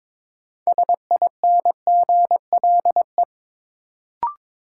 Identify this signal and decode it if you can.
Morse code, heard as a keyed tone, about 22 wpm, 700 Hz. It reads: SINGLE
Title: Sentences from Top 700 Words - New Word Drill - 22wpm